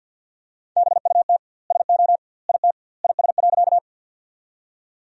Morse code, heard as a keyed tone, about 50 wpm, 700 Hz. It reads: BUT SO IT IS.
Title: 2x Word-Spacing - Sentences from Top 100 Words - 50wpm